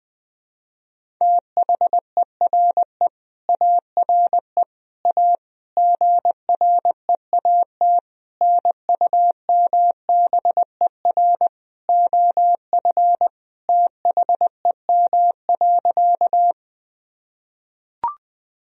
Morse code, heard as a keyed tone, about 21 wpm, 700 Hz. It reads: THERE ARE A GREAT NUMBER OF THEM.
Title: Sentences from Top 1000 Words - Review Part 1 - 20wpm